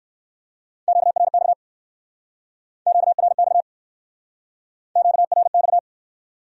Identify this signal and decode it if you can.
Morse code, heard as a keyed tone, about 64 wpm, 700 Hz. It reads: QRX QRX QRX